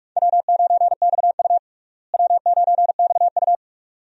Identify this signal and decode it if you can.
Morse code, heard as a keyed tone, about 43 wpm, 700 Hz. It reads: W9XU W9XU
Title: USA Call Signs - 4 Characters - Repeated 2x - 45wpm